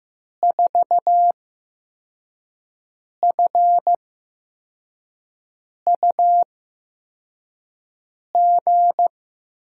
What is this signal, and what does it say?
Morse code, heard as a keyed tone, about 15 wpm, spaced as for 4 wpm, 700 Hz. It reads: 4FUG
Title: USA Call Signs - 5 Characters - 8x Character Spacing - 15wpm